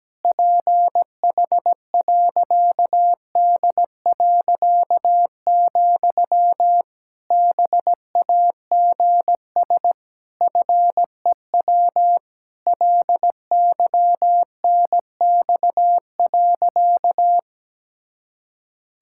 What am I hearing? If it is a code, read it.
Morse code, heard as a keyed tone, about 17 wpm, 700 Hz. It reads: PH.D., BAGS FEW LYNX.
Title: Pangram Challenge v2 - 17wpm